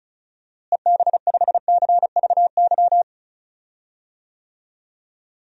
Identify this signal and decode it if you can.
Morse code, heard as a keyed tone, about 36 wpm, 700 Hz. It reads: EB5CVY
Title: DXCC-100 Call Signs - 6 Characters - Repeated 2x - 35wpm